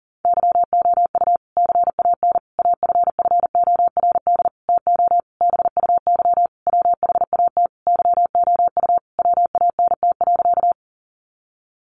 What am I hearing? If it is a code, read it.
Morse code, heard as a keyed tone, about 40 wpm, 700 Hz. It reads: YOU CAN AFFORD TO BUY WHAT YOU WANT.
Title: Sentences from Top 1100 Words 40wpm